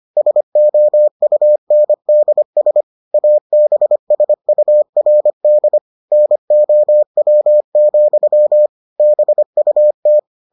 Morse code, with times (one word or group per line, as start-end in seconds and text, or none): 0.2-2.8 SOUNDS
3.1-5.8 ABSURD
6.1-8.7 NOW,
9.0-10.2 BUT